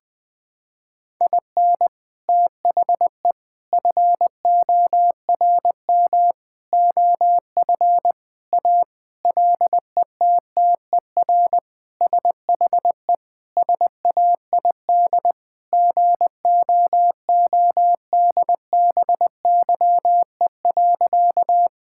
Morse code, heard as a keyed tone, about 21 wpm, 700 Hz. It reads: IN THE FORM OF A LETTER SHE SAID GOODBYE.
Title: Sentences with 4-Word Phrases - 20wpm